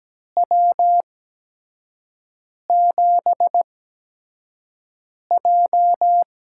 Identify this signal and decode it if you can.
Morse code, heard as a keyed tone, about 17 wpm, spaced as for 5 wpm, 700 Hz. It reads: W7J
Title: USA Call Signs - 4 Characters - 8x Character Spacing - 17wpm